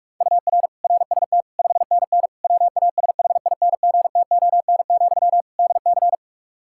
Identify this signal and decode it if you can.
Morse code, heard as a keyed tone, about 46 wpm, 700 Hz. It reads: UR RST 5NN WASHINGTON, DC
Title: Practice QSO - Exchange 1 - Beginning - 45wpm